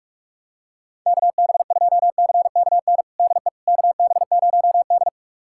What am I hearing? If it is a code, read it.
Morse code, heard as a keyed tone, about 43 wpm, 700 Hz. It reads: KB2CKN DE KB0D
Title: USA Callsign DE Callsign 45wpm